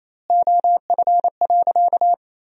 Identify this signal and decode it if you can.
Morse code, heard as a keyed tone, about 28 wpm, 700 Hz. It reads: OF.
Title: 2x Word-Spacing - Sentences from Top 300 Words - 28wpm